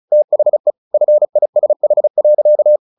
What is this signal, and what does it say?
Morse code, heard as a keyed tone, about 36 wpm, 600 Hz. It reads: THE FISH.
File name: the old man and the sea 2 35wpm